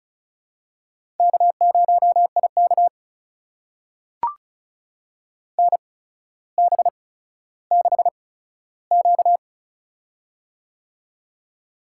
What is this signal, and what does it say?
Morse code, heard as a keyed tone, about 36 wpm, spaced as for 15 wpm, 700 Hz. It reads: K0IK NB6Q
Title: USA Call Signs - 4 Characters - 8x Character Spacing - 35wpm